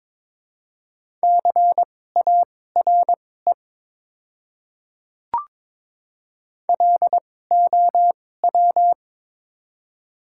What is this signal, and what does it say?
Morse code, heard as a keyed tone, about 22 wpm, spaced as for 16 wpm, 700 Hz. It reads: CARE LOW
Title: 2x Character Spacing - Top 300 Words - 22wpm